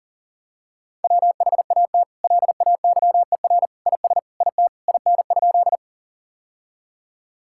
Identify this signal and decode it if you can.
Morse code, heard as a keyed tone, about 40 wpm, 700 Hz. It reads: WHAT LAYER IS IT IN?
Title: Sentences from Top 1600 Words 40wpm